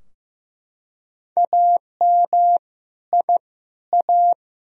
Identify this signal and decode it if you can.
Morse code, heard as a keyed tone, about 15 wpm, 700 Hz. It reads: AM I A
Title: Sentences from Top 1000 Words - Review Part 2 - 15wpm